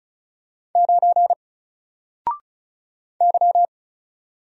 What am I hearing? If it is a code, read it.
Morse code, heard as a keyed tone, about 34 wpm, 700 Hz. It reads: 9 Y
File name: Single-Letter-Number-35wpm